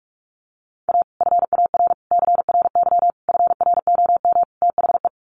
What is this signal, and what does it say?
Morse code, heard as a keyed tone, about 44 wpm, 700 Hz. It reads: A FAR CRY FROM THE
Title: Sets of 5 Words Encore 45wpm